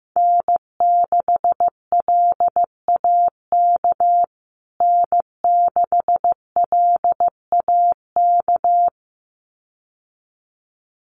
Morse code, heard as a keyed tone, about 15 wpm, 700 Hz. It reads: N6LAK N6LAK
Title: USA Call Signs Repeated 2x 15wpm